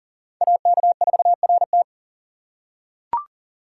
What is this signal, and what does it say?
Morse code, heard as a keyed tone, about 40 wpm, 700 Hz. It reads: AK4RT
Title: USA Call Signs - 5 Characters - 6x Character Spacing - 40wpm